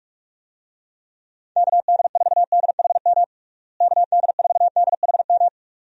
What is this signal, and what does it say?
Morse code, heard as a keyed tone, about 46 wpm, 700 Hz. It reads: KD4DHM KD4DHM
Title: USA Call Signs - 6 Characters - Repeated 2x - 45wpm